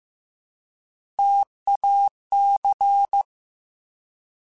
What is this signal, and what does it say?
Morse code, heard as a keyed tone, about 15 wpm, 800 Hz. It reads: TAC
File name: Alphabetical / Alphabetical mp3s / Ts @13